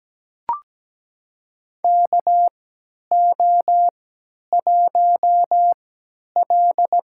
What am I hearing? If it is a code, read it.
Morse code, heard as a keyed tone, about 17 wpm, spaced as for 10 wpm, 700 Hz. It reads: KO1L
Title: USA Call Signs - 4 Characters - 3x Character Spacing - 17wpm